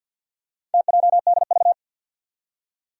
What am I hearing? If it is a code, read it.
Morse code, heard as a keyed tone, about 48 wpm, 700 Hz. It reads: TJDV
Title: Four Challenging Letter-Number Combinations 50wpm